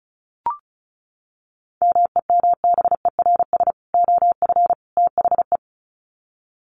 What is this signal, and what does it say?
Morse code, heard as a keyed tone, about 34 wpm, 700 Hz. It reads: MEMBERS OF THE
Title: Sets of 3 Words 35wpm